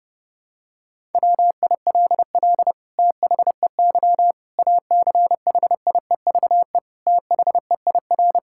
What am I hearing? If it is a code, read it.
Morse code, heard as a keyed tone, about 32 wpm, 700 Hz. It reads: WILL THEY ACHIEVE THEIR
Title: Sentences from Top 1000 Words - Review Part 6 - 30wpm